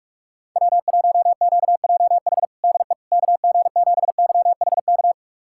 Morse code, heard as a keyed tone, about 47 wpm, 700 Hz. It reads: W1QJH DE KG7YHK
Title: USA Callsign DE Callsign 45wpm